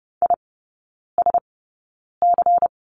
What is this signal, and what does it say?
Morse code, heard as a keyed tone, about 30 wpm, spaced as for 9 wpm, 700 Hz. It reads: ISC
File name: Lesson 19e - Letter w - Three Characters 30wpm